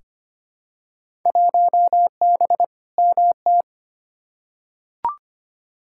Morse code, heard as a keyed tone, about 25 wpm, 700 Hz. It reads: 1B MT
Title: Field Day Exchanges - 25wpm